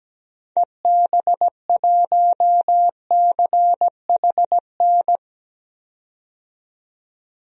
Morse code, heard as a keyed tone, about 17 wpm, 700 Hz. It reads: EB1CHN
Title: DXCC-100 Call Signs - 6 Characters - 17wpm